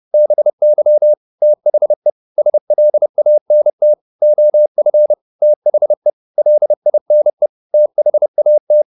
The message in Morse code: BY THE SLANT OF THE LINE THAT

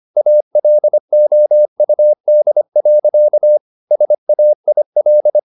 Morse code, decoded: ALOUD. SAIL

